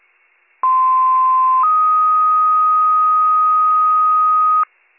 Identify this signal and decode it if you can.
Two-tone page, heard as a one-second tone, then a three-second tone, about 1040 then 1250 Hz. It is heard from 0.6 to 4.6 s.